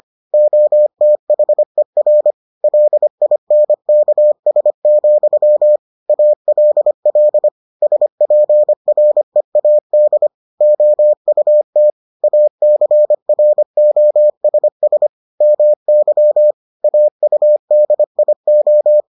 Morse code, heard as a keyed tone, about 25 wpm, 600 Hz. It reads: OTHER LINKS, ALL SPREAD OUT ACROSS MY AUDIO